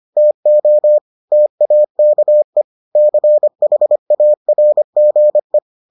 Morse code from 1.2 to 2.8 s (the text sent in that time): TAKE